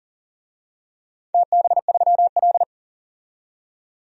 Morse code, heard as a keyed tone, about 39 wpm, 700 Hz. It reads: TB3L
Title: DXCC-100 Call Signs - 4 Characters - 40wpm